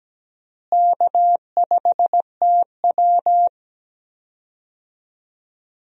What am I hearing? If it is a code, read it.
Morse code, heard as a keyed tone, about 17 wpm, 700 Hz. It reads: K5TW
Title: USA Call Signs - 4 Characters - Repeated 3x - 17wpm